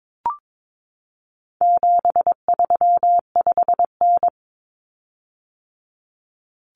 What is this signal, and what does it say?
Morse code, heard as a keyed tone, about 22 wpm, 700 Hz. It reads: 735N
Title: Common Cut Numbers - 4-Digit 22wpm